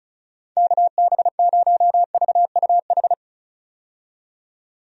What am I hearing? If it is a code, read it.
Morse code, heard as a keyed tone, about 34 wpm, 700 Hz. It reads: KB0VUH
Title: USA Call Signs - 6 Characters - 35wpm